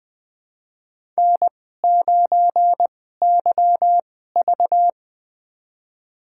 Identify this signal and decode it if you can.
Morse code, heard as a keyed tone, about 20 wpm, spaced as for 14 wpm, 700 Hz. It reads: N9YV
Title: USA Call Signs - 4 Characters - 2x Character Spacing - 20wpm